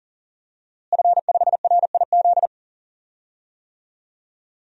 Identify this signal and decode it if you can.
Morse code, heard as a keyed tone, about 40 wpm, 700 Hz. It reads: F5RIZ
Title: DXCC-100 Call Signs - 5 Characters - Repeated 3x - 40wpm